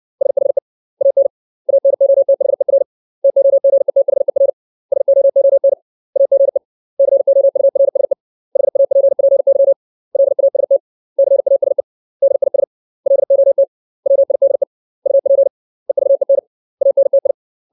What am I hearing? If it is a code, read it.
Morse code, heard as a keyed tone, about 60 wpm, 550 Hz. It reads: SHE AN ANOTHER TOGETHER SOON AGE COURSE HAPPY LAST CASE DIS LOT WIDE UP EVEN ANTI
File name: 632 words / 632 Words 60WPM